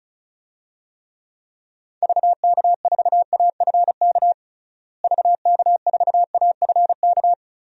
Morse code, heard as a keyed tone, about 36 wpm, 700 Hz. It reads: VK4AFK VK4AFK
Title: DXCC-100 Call Signs - 6 Characters - Repeated 3x - 35wpm